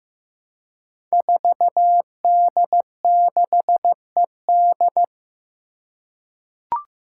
Morse code, heard as a keyed tone, about 15 wpm, 700 Hz. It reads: 4D6ED
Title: Five Challenging Letter-Number Combinations 15wpm